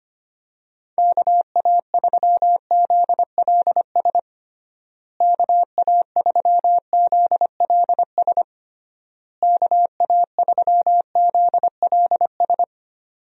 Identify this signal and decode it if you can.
Morse code, heard as a keyed tone, about 25 wpm, 700 Hz. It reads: KA3ZLS KA3ZLS KA3ZLS
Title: USA Call Signs - 6 Characters - Repeated 3x - 25wpm